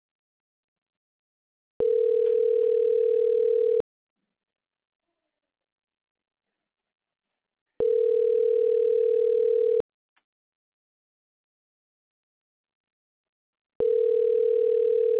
Ringback tone (North America), 440 + 480 Hz.